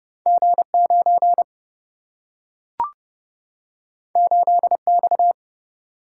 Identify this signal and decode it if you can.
Morse code, heard as a keyed tone, about 29 wpm, 700 Hz. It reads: G 9 8 X
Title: Single-Letter Single-Number 30wpm